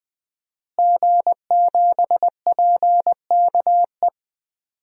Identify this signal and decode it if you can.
Morse code, heard as a keyed tone, about 20 wpm, 700 Hz. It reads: G7PKE
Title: DXCC-100 Call Signs - 5 Characters - Repeated 3x - 20wpm